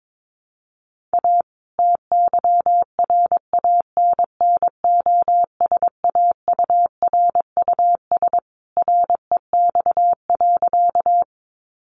Morse code, heard as a keyed tone, about 22 wpm, 700 Hz. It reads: A TYRANNOSAURUS REX.
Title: A Sound Of Thunder 22wpm